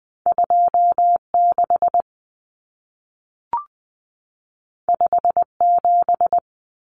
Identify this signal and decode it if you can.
Morse code, heard as a keyed tone, about 20 wpm, 700 Hz. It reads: 26 57
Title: Common Cut Numbers - 2-Digit 20wpm